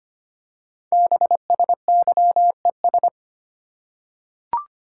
Morse code, heard as a keyed tone, about 25 wpm, 700 Hz. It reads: BSYES
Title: Code Groups - 5 Letters - 25wpm